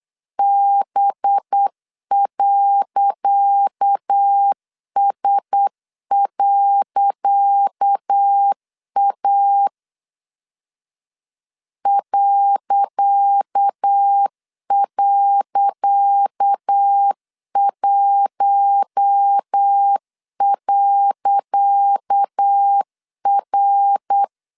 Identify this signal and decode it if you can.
Morse code, heard as a keyed tone, about 8 wpm, 800 Hz. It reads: B.S.A ..1.R
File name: LEZIONI / 10 SIGNS / 10C DOT 8wpm 3649